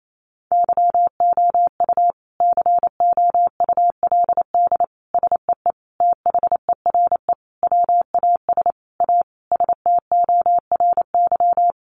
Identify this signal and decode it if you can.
Morse code, heard as a keyed tone, about 28 wpm, 700 Hz. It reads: YOU COULD SEE THERE WAS A STORY
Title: Sentences from Top 500 Words 28wpm